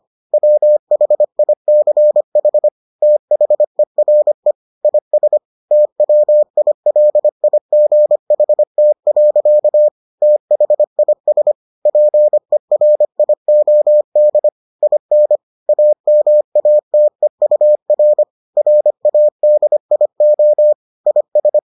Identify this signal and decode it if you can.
Morse code, heard as a keyed tone, about 25 wpm, 600 Hz. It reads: WHICH THERE IS TWILIGHT. THIS PERIOD IN AMATEUR RADIO IS